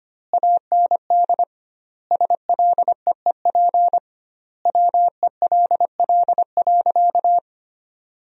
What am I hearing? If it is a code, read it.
Morse code, heard as a keyed tone, about 25 wpm, 700 Hz. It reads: AND SLEEP WELL.
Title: 2x Word-Spacing - Sentences from Top 500 Words - 25wpm